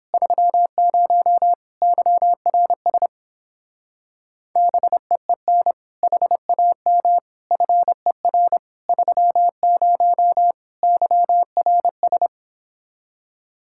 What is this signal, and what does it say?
Morse code, heard as a keyed tone, about 29 wpm, 700 Hz. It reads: 30 YRS BEEN HAM FER 30 YRS